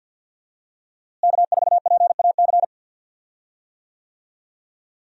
Morse code, heard as a keyed tone, about 51 wpm, 700 Hz. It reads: K4PAC
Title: USA Call Signs - 5 Characters - Repeated 3x - 50wpm